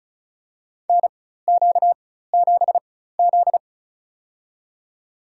Morse code, heard as a keyed tone, about 36 wpm, spaced as for 16 wpm, 700 Hz. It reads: NQ7Z
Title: USA Call Signs - 4 Characters - 4x Character Spacing - 35wpm